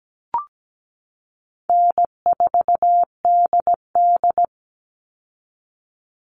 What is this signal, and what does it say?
Morse code, heard as a keyed tone, about 17 wpm, 700 Hz. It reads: N4DD
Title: Easy Call Signs Repeated 2x - 17wpm